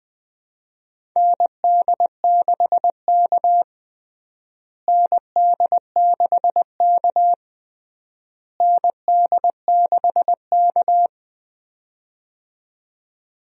Morse code, heard as a keyed tone, about 20 wpm, 700 Hz. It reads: ND6K ND6K ND6K
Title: USA Call Signs - 4 Characters - Repeated 3x - 20wpm